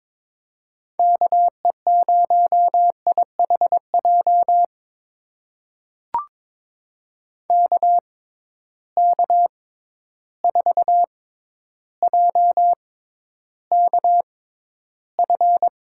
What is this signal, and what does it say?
Morse code, heard as a keyed tone, about 22 wpm, spaced as for 11 wpm, 700 Hz. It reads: KE0IHJ KK4JKF
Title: USA Call Signs - 6 Characters - 6x Character Spacing - 22wpm